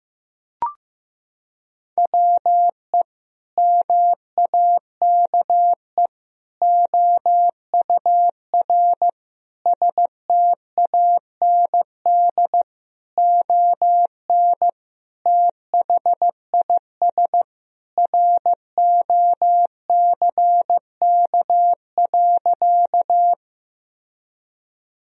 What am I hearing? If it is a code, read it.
Morse code, heard as a keyed tone, about 15 wpm, 700 Hz. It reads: WE MAKE OUR STAND ON THIS ROCK.
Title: Sentences from Top 1000 Words - Review Part 2 - 15wpm